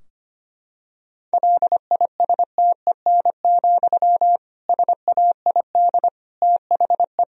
Morse code, heard as a keyed tone, about 25 wpm, 700 Hz. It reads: LISTEN, SAID THE
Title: Sentences from Top 1000 Words - Review Part 3 - 25wpm